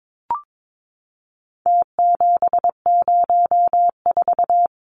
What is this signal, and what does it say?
Morse code, heard as a keyed tone, about 22 wpm, 700 Hz. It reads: T704
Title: Numbers - Contest Style - 4-Digit 22wpm